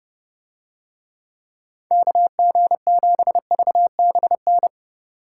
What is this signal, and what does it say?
Morse code, heard as a keyed tone, about 30 wpm, 700 Hz. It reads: KG7VBN